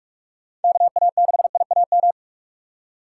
Morse code, heard as a keyed tone, about 44 wpm, 700 Hz. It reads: KA6IAM
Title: USA Call Signs - 6 Characters - 6x Character Spacing - 45wpm